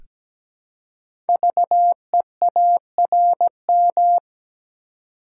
Morse code, heard as a keyed tone, about 17 wpm, 700 Hz. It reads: VEARM
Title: Code Groups - 5 Letters - 17wpm